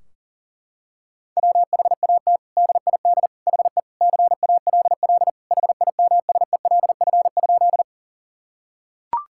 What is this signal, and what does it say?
Morse code, heard as a keyed tone, about 37 wpm, 700 Hz. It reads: WHAT DID HE CALL HIMSELF?
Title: Sentences from Top 1600 Words 40wpm